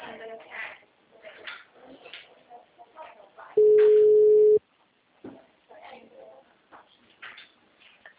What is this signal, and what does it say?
Ringback tone (Europe, 425 Hz), with speech.
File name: out- -2015- - - .9553